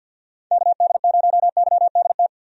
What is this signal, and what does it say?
Morse code, heard as a keyed tone, about 51 wpm, 700 Hz. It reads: KD0YDT
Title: USA Call Signs - 6 Characters - 50wpm